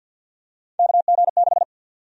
Morse code, heard as a keyed tone, about 48 wpm, 700 Hz. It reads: KG6
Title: USA Call Sign Prefixes - Repeated 2x - 50wpm